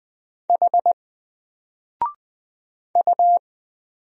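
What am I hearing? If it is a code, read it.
Morse code, heard as a keyed tone, about 20 wpm, 700 Hz. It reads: H U
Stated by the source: Single letter and number Morse drill at 20 wpm, from N4LCD.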